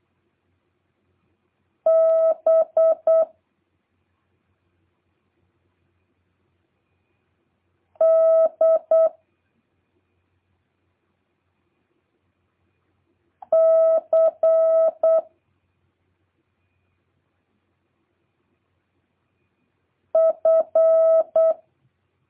Morse code, heard as a keyed tone, about 8 wpm, 650 Hz. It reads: B D C F